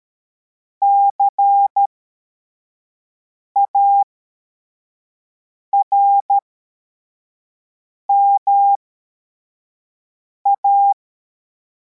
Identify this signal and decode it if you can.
Morse code, heard as a keyed tone, about 13 wpm, spaced as for 4 wpm, 800 Hz. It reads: CARMA